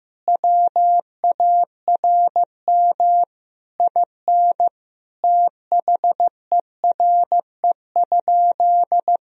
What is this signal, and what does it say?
Morse code, heard as a keyed tone, about 15 wpm, 700 Hz. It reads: WARM IN THERE?
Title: Sentences from Top 1000 Words - Review Part 4 - 15wpm